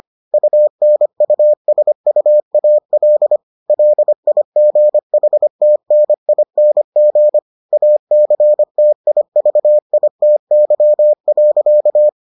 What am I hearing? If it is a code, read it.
Morse code, heard as a keyed tone, about 25 wpm, 600 Hz. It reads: UNUSUAL LIGHTNING ACTIVITY.